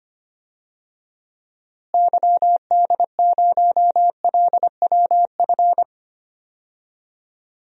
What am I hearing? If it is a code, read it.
Morse code, heard as a keyed tone, about 25 wpm, 700 Hz. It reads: YD0LWF
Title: DXCC-100 Call Signs - 6 Characters - Repeated 3x - 25wpm